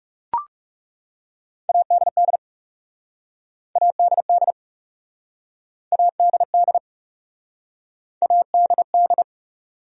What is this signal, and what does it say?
Morse code, heard as a keyed tone, about 40 wpm, 700 Hz. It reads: ADD ADD ADD ADD